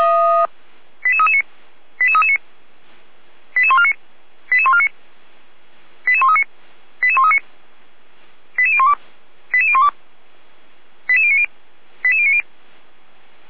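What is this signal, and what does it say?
Five-tone selective call (ZVEI), heard as quick runs of five tones, tones of about 70 ms. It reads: 88339 88339 88149 88149 88139 88139 88012 88012 88090 88090
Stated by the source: ZVEI five-tone call received by SDR, fire brigade feuerwehr-orlamuende.org.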